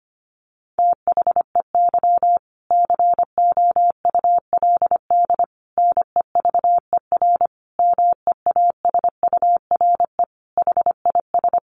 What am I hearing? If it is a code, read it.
Morse code, heard as a keyed tone, about 25 wpm, 700 Hz. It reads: THEY COULD NEVER MEASURE HIS
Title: Sentences from Top 1200 Words 25wpm